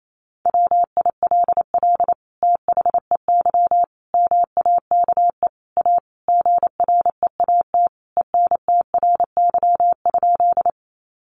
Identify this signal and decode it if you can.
Morse code, heard as a keyed tone, about 28 wpm, 700 Hz. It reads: WILL THEY MAKE A GREAT ENTRY?
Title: Sentences from Top 1500 Words 28wpm